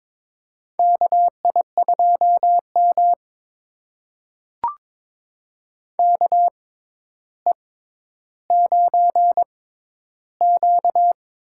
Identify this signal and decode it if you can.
Morse code, heard as a keyed tone, about 21 wpm, spaced as for 11 wpm, 700 Hz. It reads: KI2M KE9Q
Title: USA Call Signs - 4 Characters - 6x Character Spacing - 22wpm